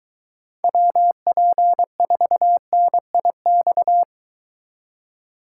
Morse code, heard as a keyed tone, about 23 wpm, 700 Hz. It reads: WP4NIX